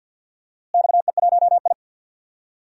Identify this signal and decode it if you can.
Morse code, heard as a keyed tone, about 49 wpm, 700 Hz. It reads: XE1I
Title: DXCC-100 Call Signs - 4 Characters - Repeated 3x - 50wpm